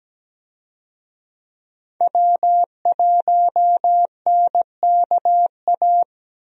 Morse code, heard as a keyed tone, about 17 wpm, 700 Hz. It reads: W1NKA
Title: USA Call Signs - 5 Characters - Repeated 3x - 17wpm